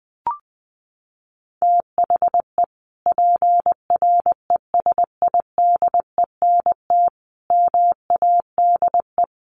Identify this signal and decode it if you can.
Morse code, heard as a keyed tone, about 20 wpm, 700 Hz. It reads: THE PRESIDENT MADE
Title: Sentences from Top 1400 Words 20wpm